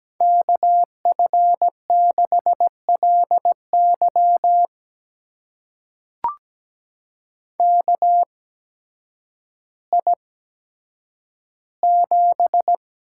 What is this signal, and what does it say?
Morse code, heard as a keyed tone, about 17 wpm, spaced as for 9 wpm, 700 Hz. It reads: KF6LY KI7
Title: USA Call Signs - 8x Character Spacing - 17wpm